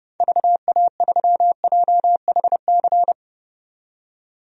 Morse code, heard as a keyed tone, about 32 wpm, 700 Hz. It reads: VA3JHC
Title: DXCC-100 Call Signs - 6 Characters - 30wpm